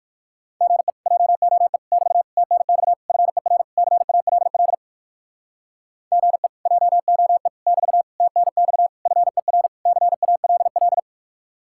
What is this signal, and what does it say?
Morse code, heard as a keyed tone, about 58 wpm, 700 Hz. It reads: GE JOE <BT> TNX FER CALL GE JOE <BT> TNX FER CALL